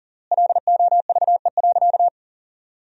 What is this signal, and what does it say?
Morse code, heard as a keyed tone, about 40 wpm, 700 Hz. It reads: LOVE.
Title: 3x Word-Spacing - Sentences from Top 400 Words - 40wpm